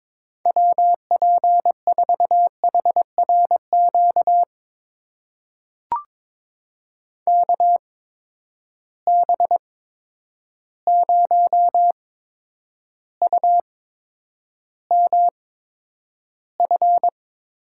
Morse code, heard as a keyed tone, about 23 wpm, spaced as for 10 wpm, 700 Hz. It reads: WP4HRQ KB0UMF
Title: USA Call Signs - 6 Characters - 8x Character Spacing - 22wpm